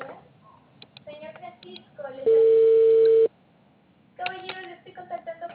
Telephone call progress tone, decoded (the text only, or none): ringback tone